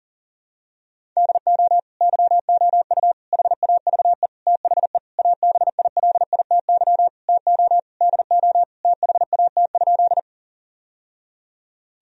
Morse code, heard as a keyed tone, about 37 wpm, 700 Hz. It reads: DO YOU HAVE THE ABILITY TO DO THAT?
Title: Sentences from Top 1000 Words - Review Part 3 - 40wpm